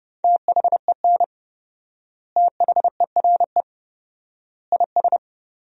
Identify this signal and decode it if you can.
Morse code, heard as a keyed tone, about 30 wpm, 700 Hz. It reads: THEN THERE IS
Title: 4x Word-Spacing - Sentences from Top 100 Words - 30wpm